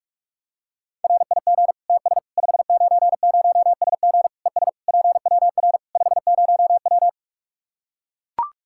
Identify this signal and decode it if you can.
Morse code, heard as a keyed tone, about 41 wpm, 700 Hz. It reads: RIG TS 590SG ES PWR 50W
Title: Practice QSO - Exchange 2 - Info - 45wpm